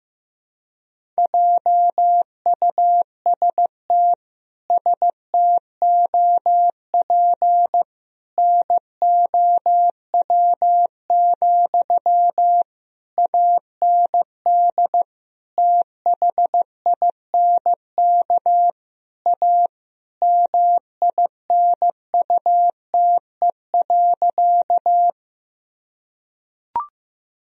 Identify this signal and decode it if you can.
Morse code, heard as a keyed tone, about 14 wpm, 700 Hz. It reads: JUST STOP NOW, AND THINK A MINUTE.
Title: Sentences from Top 1000 Words - Review Part 1 - 15wpm